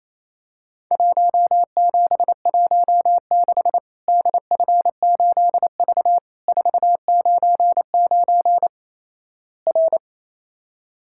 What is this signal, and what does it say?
Morse code, heard as a keyed tone, about 27 wpm, 700 Hz. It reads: 1716 DF8V 499 R
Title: Worked All Europe DX Contest - QTC Exchange - 28wpm